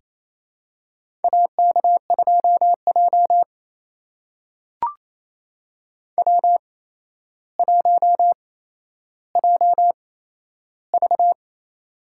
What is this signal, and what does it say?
Morse code, heard as a keyed tone, about 28 wpm, spaced as for 12 wpm, 700 Hz. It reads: AK2J W1JV
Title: USA Call Signs - 4 Characters - 8x Character Spacing - 28wpm